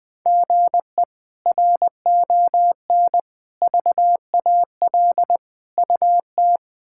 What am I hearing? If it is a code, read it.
Morse code, heard as a keyed tone, about 20 wpm, 700 Hz. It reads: GE RON VAL UT
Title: K1USN SST QSO Practice - 20wpm